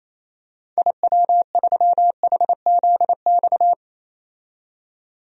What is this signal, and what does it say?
Morse code, heard as a keyed tone, about 28 wpm, 700 Hz. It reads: IW3HZX
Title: DXCC-100 Call Signs - 6 Characters - 28wpm